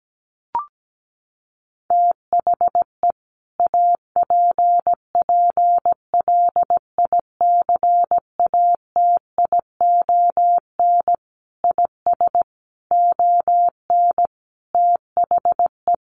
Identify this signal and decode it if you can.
Morse code, heard as a keyed tone, about 17 wpm, 700 Hz. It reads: THE APPLICATION IS ON THE
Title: Sentences from Top 1200 Words 17wpm